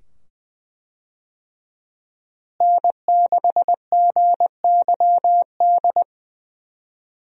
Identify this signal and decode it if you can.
Morse code, heard as a keyed tone, about 20 wpm, 700 Hz. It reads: N6GYD
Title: USA Call Signs - 5 Characters - Repeated 2x - 20wpm